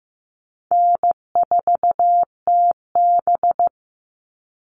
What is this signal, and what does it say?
Morse code, heard as a keyed tone, about 15 wpm, 700 Hz. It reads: N4TB
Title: Call Signs Repeated 2x - 15wpm